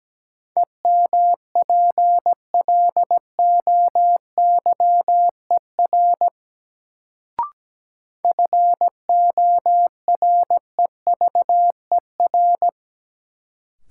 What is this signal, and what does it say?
Morse code, heard as a keyed tone, about 17 wpm, 700 Hz. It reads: EMPLOYER FOREVER
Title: Sentences from Top 1600 Words - New Word Drill - 17wpm